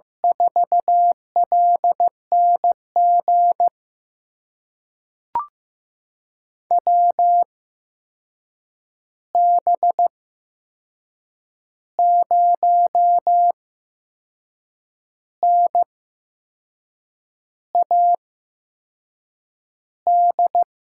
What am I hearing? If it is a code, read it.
Morse code, heard as a keyed tone, about 15 wpm, spaced as for 6 wpm, 700 Hz. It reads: 4LNG WB0NAD
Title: USA Call Signs - 6 Characters - 8x Character Spacing - 15wpm